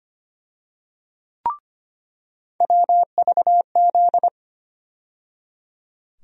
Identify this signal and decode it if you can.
Morse code, heard as a keyed tone, about 25 wpm, 700 Hz. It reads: WVZ
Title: USA Call Sign Postfixes - 3 Letters 25wpm